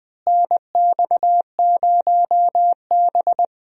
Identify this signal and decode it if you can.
Morse code, heard as a keyed tone, about 20 wpm, 700 Hz. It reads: NX0B
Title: USA Call Signs - 4 Characters - Repeated 3x - 20wpm